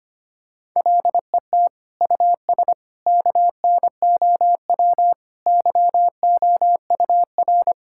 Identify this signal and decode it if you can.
Morse code, heard as a keyed tone, about 25 wpm, 700 Hz. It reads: LET US KNOW YOUR
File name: practice / 25wpm / Sentences with 9-Word Phrases - 25wpm